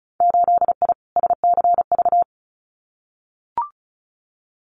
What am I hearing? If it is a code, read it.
Morse code, heard as a keyed tone, about 34 wpm, 700 Hz. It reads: 8I SCV
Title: Winter Field Day Exchanges Encore 35wpm